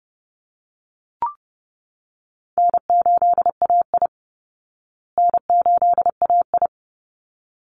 Morse code, heard as a keyed tone, about 30 wpm, 700 Hz. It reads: N8AI N8AI
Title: Easy Call Signs Repeated 2x - 30wpm